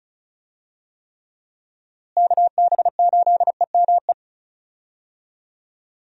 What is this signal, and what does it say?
Morse code, heard as a keyed tone, about 36 wpm, 700 Hz. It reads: KB8EME
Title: USA Call Signs - 6 Characters - Repeated 3x - 35wpm